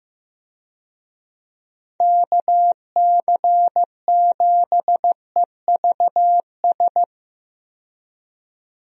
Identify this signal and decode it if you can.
Morse code, heard as a keyed tone, about 15 wpm, 700 Hz. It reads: KC7EVS
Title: USA Call Signs - 6 Characters - 15wpm